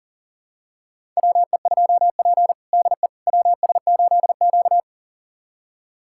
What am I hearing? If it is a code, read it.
Morse code, heard as a keyed tone, about 40 wpm, 700 Hz. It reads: WE2P DE WS8Q
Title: USA Callsign DE Callsign - 4 Characters - 40wpm